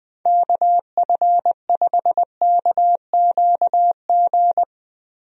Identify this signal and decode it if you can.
Morse code, heard as a keyed tone, about 20 wpm, 700 Hz. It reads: KF5KQG
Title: USA Call Signs - 6 Characters - Repeated 2x - 20wpm